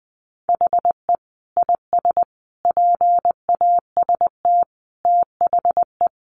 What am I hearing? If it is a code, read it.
Morse code, heard as a keyed tone, about 20 wpm, 700 Hz. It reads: HE IS PAST THE
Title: Sentences from Top 1400 Words 20wpm